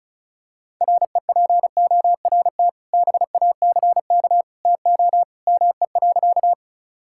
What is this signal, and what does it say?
Morse code, heard as a keyed tone, about 34 wpm, 700 Hz. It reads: REPORT BACK TO ME.
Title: Sentences from Top 1000 Words - Review Part 6 - 35wpm